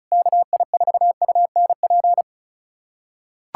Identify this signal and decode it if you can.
Morse code, heard as a keyed tone, about 34 wpm, 700 Hz. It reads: KI4UNP
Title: USA Callsign DE Callsign - 6 Characters - 35wpm